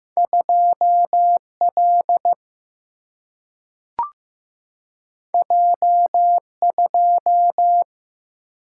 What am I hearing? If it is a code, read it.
Morse code, heard as a keyed tone, about 15 wpm, 700 Hz. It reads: 2 L J 2
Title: Single-Letter Single-Number 15wpm